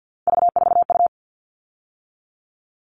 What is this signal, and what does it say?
Morse code, heard as a keyed tone, about 50 wpm, 700 Hz. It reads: V4U
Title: All Cut Numbers - 3-Digit 50wpm